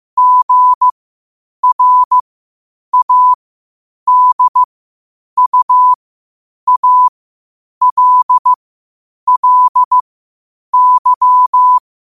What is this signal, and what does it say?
Morse code, heard as a keyed tone, about 15 wpm, spaced as for 8 wpm, 1000 Hz. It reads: GRADUALLY
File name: qotdfiles / 8wpm